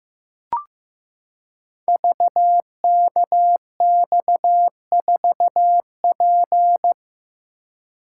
Morse code, heard as a keyed tone, about 15 wpm, 700 Hz. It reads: VKX4P
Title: Code Groups - 5 Letter:Numbers - 15wpm